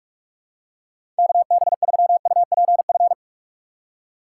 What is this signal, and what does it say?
Morse code, heard as a keyed tone, about 46 wpm, 700 Hz. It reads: KB3UPF
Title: USA Call Signs - 6 Characters - 2x Character Spacing - 45wpm